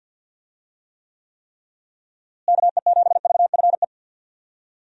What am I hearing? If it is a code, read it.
Morse code, heard as a keyed tone, about 52 wpm, 700 Hz. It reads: KE7VFE
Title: USA Call Signs - 6 Characters - Repeated 3x - 50wpm